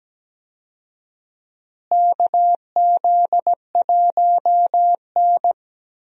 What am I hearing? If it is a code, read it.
Morse code, heard as a keyed tone, about 17 wpm, 700 Hz. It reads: KZ1N
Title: USA Call Signs - 4 Characters - Repeated 2x - 17wpm